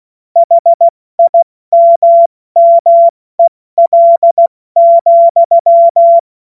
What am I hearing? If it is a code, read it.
Morse code, heard as a keyed tone, about 16 wpm, 680 Hz. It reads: HIMMEL,
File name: Morsekurs / Witze / Witz 16. Februar 2021, 70 F80